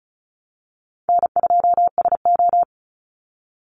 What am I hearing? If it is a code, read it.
Morse code, heard as a keyed tone, about 34 wpm, 700 Hz. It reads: N2SO
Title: Call Signs Repeated 2x - 35wpm